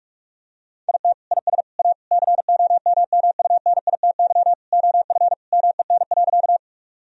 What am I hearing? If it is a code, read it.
Morse code, heard as a keyed tone, about 46 wpm, 700 Hz. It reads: IT IS A COMMUNITY OF MEN.
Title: Sentences from Top 1000 Words - Review Part 4 - 45wpm